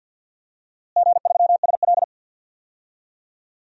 Morse code, heard as a keyed tone, about 52 wpm, 700 Hz. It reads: G3SL
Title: DXCC-100 Call Signs - 4 Characters - Repeated 3x - 50wpm